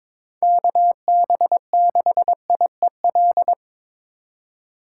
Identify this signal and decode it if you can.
Morse code, heard as a keyed tone, about 26 wpm, 700 Hz. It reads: KB6IEL
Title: Speed Racing - USA 2x3 Call Signs 22wpm